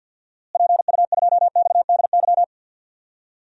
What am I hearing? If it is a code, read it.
Morse code, heard as a keyed tone, about 52 wpm, 700 Hz. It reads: PU2XDY
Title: DXCC-100 Call Signs - 6 Characters - Repeated 2x - 50wpm